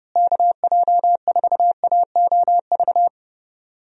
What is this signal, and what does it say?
Morse code, heard as a keyed tone, about 30 wpm, 700 Hz. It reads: KJ4AOV
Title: USA Call Signs - 6 Characters - Repeated 2x - 30wpm